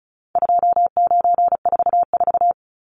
Morse code, heard as a keyed tone, about 34 wpm, 700 Hz. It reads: 2944
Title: Numbers - 4-Digit 35wpm